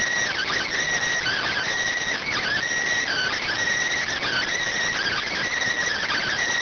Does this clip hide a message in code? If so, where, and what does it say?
Yes: Emergency Alert System (SAME) tones (header bursts) from 1.2 to 5.5 s, heard as harsh buzzing bursts.